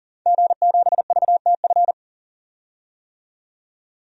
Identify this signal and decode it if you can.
Morse code, heard as a keyed tone, about 40 wpm, 700 Hz. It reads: G7VTF
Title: DXCC-100 Call Signs - 5 Characters - 40wpm